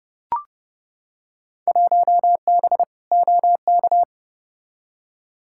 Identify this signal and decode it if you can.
Morse code, heard as a keyed tone, about 32 wpm, 700 Hz. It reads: 1B OK
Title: Field Day Exchanges Encore - 30wpm